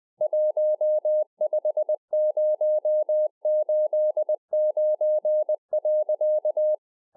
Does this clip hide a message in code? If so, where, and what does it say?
Yes: Morse code from 0.2 to 6.7 s, heard as a keyed tone, about 20 wpm, 610 Hz. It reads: 15089.